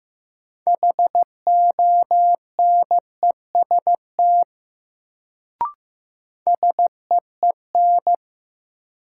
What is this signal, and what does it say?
Morse code, heard as a keyed tone, about 15 wpm, 700 Hz. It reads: HONEST SEEN
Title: Sets of 5 Words Encore - New Word Drill - 15wpm